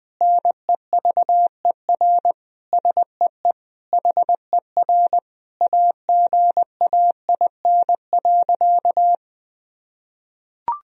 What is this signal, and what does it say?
Morse code, heard as a keyed tone, about 20 wpm, 700 Hz. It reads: NEVER SEE HER AGAIN.
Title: Sentences from Top 1000 Words - Review Part 4 - 20wpm